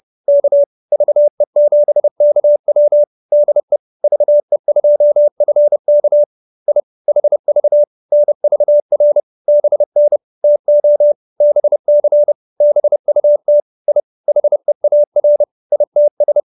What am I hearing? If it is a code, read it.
Morse code, heard as a keyed tone, about 30 wpm, 590 Hz. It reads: K VE7KW DE VE2FK I HV NVR BN TO BC BUT I HEAR ITS
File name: QSO314 30